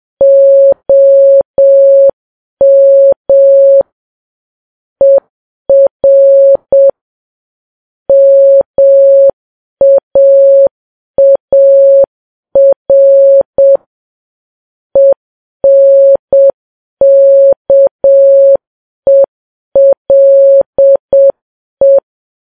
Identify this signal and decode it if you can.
Morse code, heard as a keyed tone, about 7 wpm, 550 Hz. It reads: OM ER MAAR ENKELE
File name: Files Web tls / Morse on9fv / morse7kps